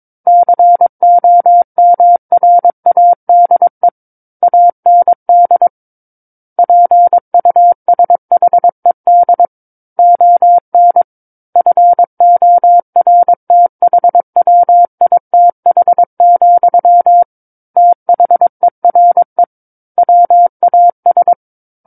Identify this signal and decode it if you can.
Morse code, heard as a keyed tone, about 22 wpm, 700 Hz. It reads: COMRADE AND PUSHED ON FORTHWITH, THERE WAS
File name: Book One / 13-Chapter 13 - 22 WPM